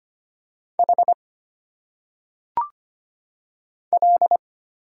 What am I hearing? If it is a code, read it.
Morse code, heard as a keyed tone, about 25 wpm, 700 Hz. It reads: H L
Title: Single Letters - Phonetic Pronunciation 25wpm